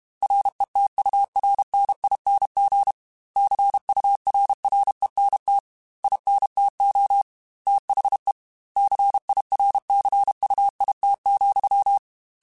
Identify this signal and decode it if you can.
Morse code, heard as a keyed tone, about 32 wpm, 800 Hz. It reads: RETURNING CURRENT INTO THE CIRCUIT,
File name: Technical Subjects / Technical Subjects-mp3s / XC @30